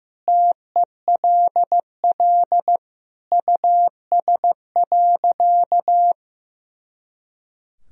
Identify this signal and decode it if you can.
Morse code, heard as a keyed tone, about 15 wpm, 700 Hz. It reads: TELL US.
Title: Sentences from Top 400 Words and Abbreviations 15wpm